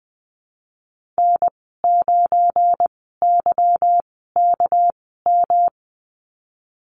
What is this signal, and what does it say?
Morse code, heard as a keyed tone, about 20 wpm, spaced as for 15 wpm, 700 Hz. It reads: N9YKM
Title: 238 Lesson 38g - Letter q - Call Signs 20wpm